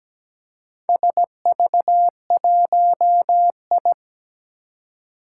Morse code, heard as a keyed tone, about 17 wpm, 700 Hz. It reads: SV1I
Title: Four Challenging Letter-Number Combinations 17wpm